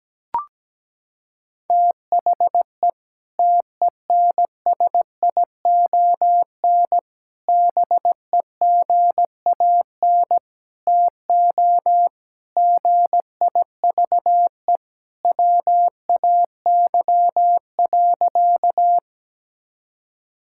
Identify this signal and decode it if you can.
Morse code, heard as a keyed tone, about 17 wpm, 700 Hz. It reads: THE TENSION BEGAN TO GIVE WAY.
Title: Sentences from Top 1600 Words 17wpm